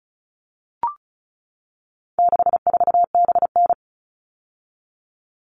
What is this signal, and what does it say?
Morse code, heard as a keyed tone, about 34 wpm, 700 Hz. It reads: 64BN
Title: All Cut Numbers - 4-Digit 35wpm